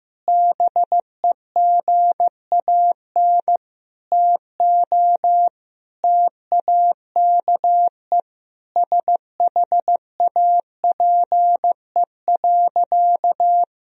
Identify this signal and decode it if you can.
Morse code, heard as a keyed tone, about 15 wpm, 700 Hz. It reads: BEGAN TO TAKE SHAPE.
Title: Sentences from Top 1000 Words - Review Part 3 - 15wpm